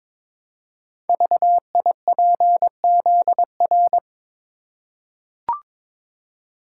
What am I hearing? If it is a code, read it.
Morse code, heard as a keyed tone, about 23 wpm, 700 Hz. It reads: VIPZR
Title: Code Groups - 5 Letters - 22wpm